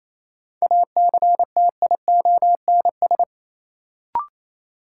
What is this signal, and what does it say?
Morse code, heard as a keyed tone, about 28 wpm, 700 Hz. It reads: ACTIONS
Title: Sentences from Top 1000 Words - Review Part 6 - New Word Drill - 28wpm